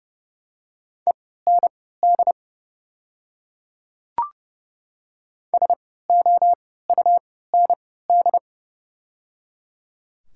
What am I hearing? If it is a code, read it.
Morse code, heard as a keyed tone, about 30 wpm, spaced as for 17 wpm, 700 Hz. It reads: END SOUND
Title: 3x Character Spacing - Top 300 Words - 30wpm